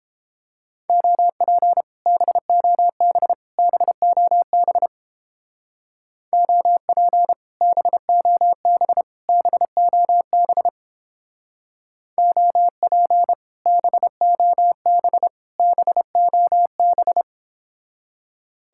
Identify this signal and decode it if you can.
Morse code, heard as a keyed tone, about 31 wpm, 700 Hz. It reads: OP BOB BOB OP BOB BOB OP BOB BOB